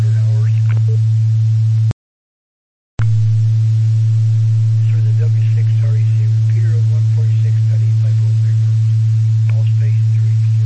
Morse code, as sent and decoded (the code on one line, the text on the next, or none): none